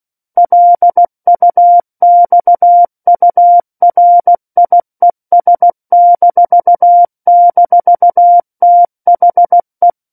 Morse code, heard as a keyed tone, about 16 wpm, 700 Hz. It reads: LUXURIES--THE
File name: Book One / 07-Chapter 7 - 16 WPM